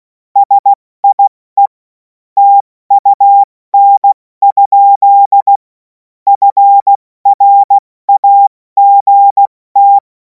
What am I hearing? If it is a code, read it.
Morse code, heard as a keyed tone, about 16 wpm, 800 Hz. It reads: SIE TUN? FRAGT
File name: Morsekurs / Witze / Witz 26. Mai 2026 70 F80